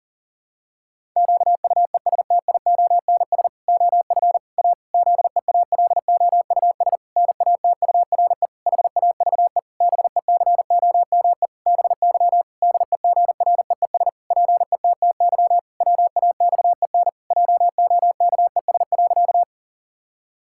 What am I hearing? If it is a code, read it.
Morse code, heard as a keyed tone, about 40 wpm, 700 Hz. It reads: QUESTIONS OF A ZEALOUS NATURE HAVE BECOME BY DEGREES PETTY WAXEN JOKES.
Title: Pangram Challenge v2 - 40wpm